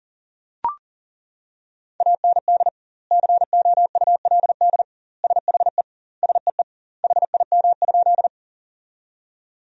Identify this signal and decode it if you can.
Morse code, heard as a keyed tone, about 40 wpm, 700 Hz. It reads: AND COULD SHE SEE HIM?
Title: 2x Word-Spacing - Sentences from Top 100 Words - 40wpm